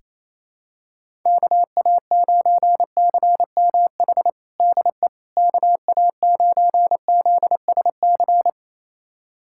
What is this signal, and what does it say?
Morse code, heard as a keyed tone, about 28 wpm, 700 Hz. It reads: KA9CMH DE KA9ZSC
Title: USA Callsign DE Callsign 28wpm